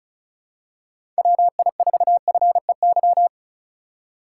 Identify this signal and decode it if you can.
Morse code, heard as a keyed tone, about 36 wpm, 690 Hz. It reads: WI4FEY
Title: USA Call Signs - 6 Characters - 6x Character Spacing - 35wpm